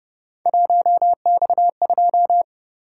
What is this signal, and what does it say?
Morse code, heard as a keyed tone, about 31 wpm, 700 Hz. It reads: 1X2